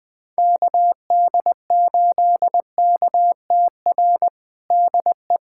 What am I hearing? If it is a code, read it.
Morse code, heard as a keyed tone, about 20 wpm, 700 Hz. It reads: KD8KTR DE
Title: USA Callsign DE Callsign - 6 Characters - 20wpm